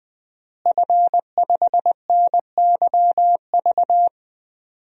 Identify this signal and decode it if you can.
Morse code, heard as a keyed tone, about 21 wpm, 700 Hz. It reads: F5NYV
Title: DXCC-100 Call Signs - 5 Characters - Repeated 3x - 20wpm